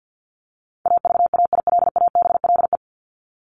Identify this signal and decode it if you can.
Morse code, heard as a keyed tone, about 50 wpm, 700 Hz. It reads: AVAILABLE
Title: Top 501-750 Words 50wpm